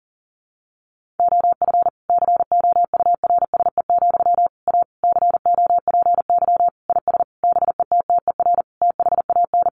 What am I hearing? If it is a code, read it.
Morse code, heard as a keyed tone, about 40 wpm, 700 Hz. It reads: OF COURSE, A COPY IS BETTER THAN
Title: Sentences from Top 1400 Words 40wpm